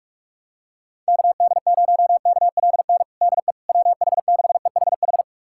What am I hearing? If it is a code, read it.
Morse code, heard as a keyed tone, about 46 wpm, 700 Hz. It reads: KD0KLN DE WH6EHH
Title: USA Callsign DE Callsign - 6 Characters - 45wpm